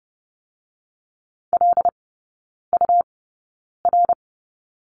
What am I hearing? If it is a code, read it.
Morse code, heard as a keyed tone, about 30 wpm, spaced as for 9 wpm, 690 Hz. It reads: LUR
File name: Lesson 17e - Letter u - Three Characters 30wpm